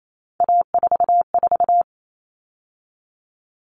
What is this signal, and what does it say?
Morse code, heard as a keyed tone, about 28 wpm, 700 Hz. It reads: A44
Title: All Cut Numbers - 3-Digit 28wpm